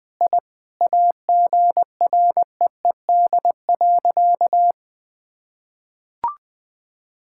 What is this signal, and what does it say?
Morse code, heard as a keyed tone, about 21 wpm, 700 Hz. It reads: I AGREED.
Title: Sentences with 8-Word Phrases - 20wpm